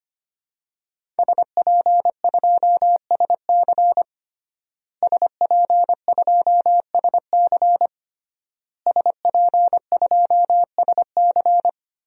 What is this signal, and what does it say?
Morse code, heard as a keyed tone, about 25 wpm, 700 Hz. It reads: SP2SC SP2SC SP2SC
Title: DXCC-100 Call Signs - 5 Characters - Repeated 3x - 25wpm